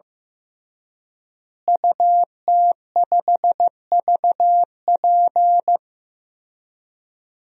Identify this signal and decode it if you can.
Morse code, heard as a keyed tone, about 15 wpm, 700 Hz. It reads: UT5VP
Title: DXCC-100 Call Signs - 5 Characters - Repeated 3x - 15wpm